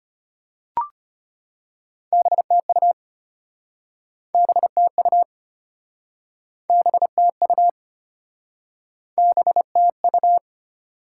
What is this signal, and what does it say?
Morse code, heard as a keyed tone, about 34 wpm, 690 Hz. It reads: BTU BTU BTU BTU